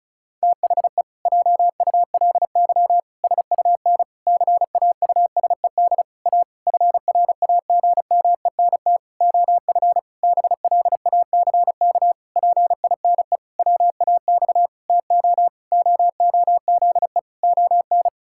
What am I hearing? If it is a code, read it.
Morse code, heard as a keyed tone, about 36 wpm, 700 Hz. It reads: THE JULY SUN CAUSED A FRAGMENT OF BLACK PINE WAX TO OOZE ON
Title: Pangram Challenge v2 - 35wpm